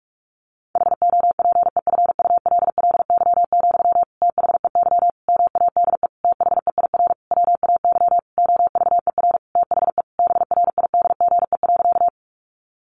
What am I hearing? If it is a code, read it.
Morse code, heard as a keyed tone, about 44 wpm, 700 Hz. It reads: HOPEFULLY, THEY MADE THEIR WAY OVER THE BRIDGE.
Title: Sentences from Top 1500 Words 45wpm